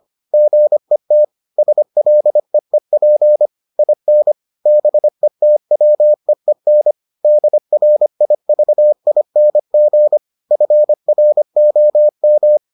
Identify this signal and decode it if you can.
Morse code, heard as a keyed tone, about 25 wpm, 600 Hz. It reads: GET SLEEP IN BETWEEN DRIVING FROM